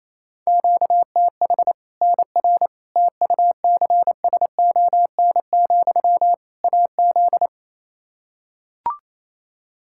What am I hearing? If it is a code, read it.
Morse code, heard as a keyed tone, about 28 wpm, 700 Hz. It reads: QTH NR TUCSON, AZ
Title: K1USN SST QSO Practice 28wpm